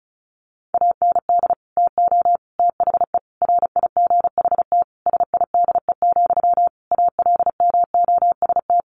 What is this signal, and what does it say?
Morse code, heard as a keyed tone, about 34 wpm, 700 Hz. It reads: AND TO THE RIGHT SIDE, ALMOST
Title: Sentences from Top 1300 Words 35wpm